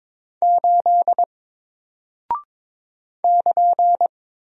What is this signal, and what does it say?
Morse code, heard as a keyed tone, about 22 wpm, 700 Hz. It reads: 8 <KN>
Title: Single Character (Letter:Number:Punctuation:Prosign) Alt 22wpm